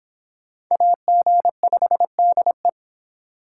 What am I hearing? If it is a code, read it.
Morse code, heard as a keyed tone, about 26 wpm, 700 Hz. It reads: AG5DE